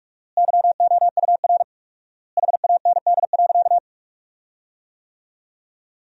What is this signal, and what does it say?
Morse code, heard as a keyed tone, about 43 wpm, 700 Hz. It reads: YOUR HAND.
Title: 4x Word-Spacing - Sentences from Top 400 Words - 45wpm